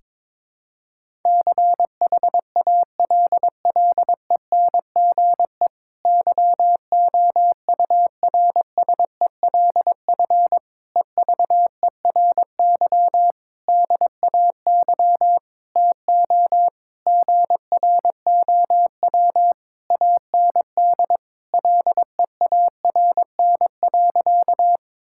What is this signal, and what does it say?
Morse code, heard as a keyed tone, about 22 wpm, 700 Hz. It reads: CHALLENGE YOURSELF EVERY DAY TO GROW AND LEARN.
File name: Difficult Sentences with Hints - 22wpm